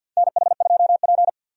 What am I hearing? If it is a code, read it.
Morse code, heard as a keyed tone, about 53 wpm, 700 Hz. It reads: NH2P
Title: USA Call Signs - 4 Characters - 50wpm